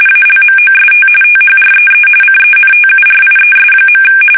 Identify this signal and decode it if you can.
RTTY, about 45.45 baud. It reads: ALMOST THREE CENTURIES.
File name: fsk850 / battleshipnj